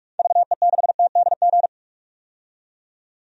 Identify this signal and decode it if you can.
Morse code, heard as a keyed tone, about 46 wpm, 700 Hz. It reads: VE6TDG
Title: DXCC-100 Call Signs - 6 Characters - 45wpm